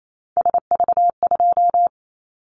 Morse code, heard as a keyed tone, about 28 wpm, 700 Hz. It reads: SV2
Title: Callsign Prefixes for Top 50 DXCC - 28wpm